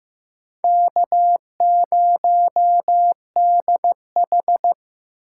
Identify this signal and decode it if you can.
Morse code, heard as a keyed tone, about 15 wpm, 700 Hz. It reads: K0DH
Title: USA Call Signs - 4x Character Spacing - 15wpm